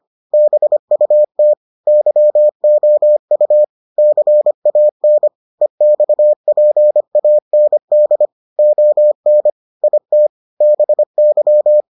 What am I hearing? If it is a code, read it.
Morse code, heard as a keyed tone, about 25 wpm, 600 Hz. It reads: BUT YOU CAN EXPAND ON IT BY